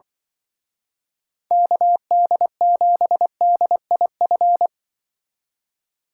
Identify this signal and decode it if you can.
Morse code, heard as a keyed tone, about 23 wpm, 700 Hz. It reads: KD7DIF
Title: Speed Racing - USA 2x3 Call Signs 20wpm